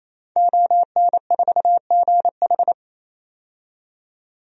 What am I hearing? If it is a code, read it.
Morse code, heard as a keyed tone, about 28 wpm, 700 Hz. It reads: ON4GH
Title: DXCC-100 Call Signs - 5 Characters - 28wpm